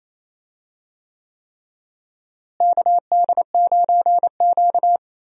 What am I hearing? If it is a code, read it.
Morse code, heard as a keyed tone, about 28 wpm, 700 Hz. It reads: KD9Q
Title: USA Call Signs - 4 Characters - Repeated 3x - 28wpm